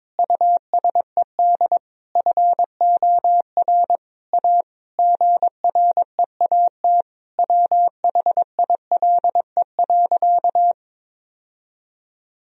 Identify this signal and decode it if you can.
Morse code, heard as a keyed tone, about 21 wpm, 700 Hz. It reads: USED FOR A GREAT WHILE.
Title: Sentences from Top 1000 Words - Review Part 1 - 22wpm